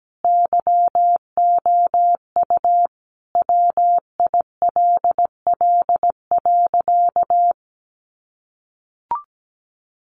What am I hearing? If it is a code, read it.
Morse code, heard as a keyed tone, about 17 wpm, 700 Hz. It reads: YOU WILL.
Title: Sentences from Top 1400 Words 17wpm